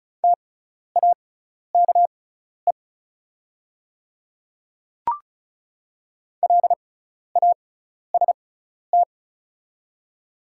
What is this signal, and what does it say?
Morse code, heard as a keyed tone, about 36 wpm, spaced as for 12 wpm, 690 Hz. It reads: TAKE LAST
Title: 6x Character Spacing - Top 500 Words - 35wpm